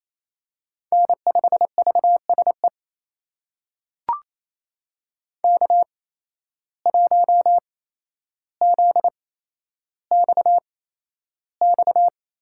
Morse code, heard as a keyed tone, about 28 wpm, spaced as for 12 wpm, 700 Hz. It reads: N5VSE K1ZXX
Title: USA Call Signs - 5 Characters - 8x Character Spacing - 28wpm